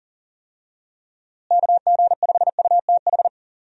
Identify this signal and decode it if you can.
Morse code, heard as a keyed tone, about 40 wpm, 700 Hz. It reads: KG5UTH
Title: USA Call Signs - 6 Characters - Repeated 3x - 40wpm